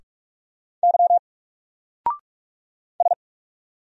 Morse code, heard as a keyed tone, about 44 wpm, 700 Hz. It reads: Y S
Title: Single Letter-Number Alt 45wpm